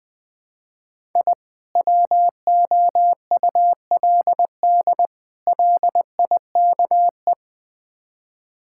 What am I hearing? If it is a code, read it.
Morse code, heard as a keyed tone, about 20 wpm, 690 Hz. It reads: I WOULD LIKE
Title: Build A Phrase - 4-Word Phrases - 20wpm